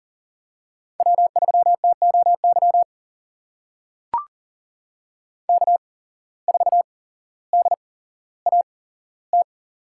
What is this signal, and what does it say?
Morse code, heard as a keyed tone, about 38 wpm, spaced as for 17 wpm, 700 Hz. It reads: W3TOY K4DAT
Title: USA Call Signs - 5 Characters - 8x Character Spacing - 40wpm